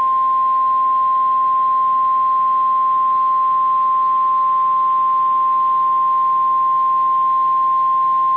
Emergency Alert System (SAME) tones (the 1050 Hz attention tone). It is heard from 0.9 to 8.1 s.
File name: EAS Logs / KMNR